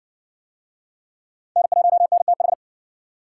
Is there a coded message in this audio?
Yes: Morse code from 1.6 to 2.5 s, heard as a keyed tone, about 60 wpm, 700 Hz. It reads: N1NTH